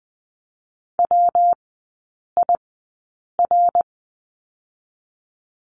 Morse code, heard as a keyed tone, about 19 wpm, 700 Hz. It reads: W I R
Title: 114 Lesson 19e - Letter w - Three Characters 20wpm